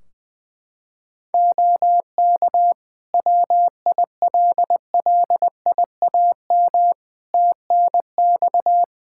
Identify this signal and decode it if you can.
Morse code, heard as a keyed tone, about 20 wpm, 700 Hz. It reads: OK WILLIAM TNX
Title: Practice QSO - Exchange 4 - Ending - 20wpm